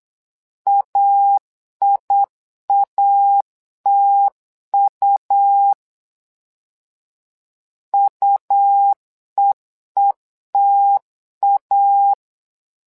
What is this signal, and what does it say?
Morse code, heard as a keyed tone, about 8 wpm, 800 Hz. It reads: AIATU UEETA
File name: LEZIONI / 2 UHDN / 2 U 8wpm 2028